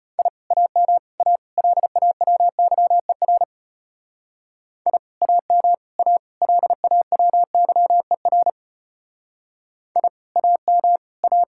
Morse code, heard as a keyed tone, about 33 wpm, 700 Hz. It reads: I AM A LAWYER I AM A LAWYER I AM A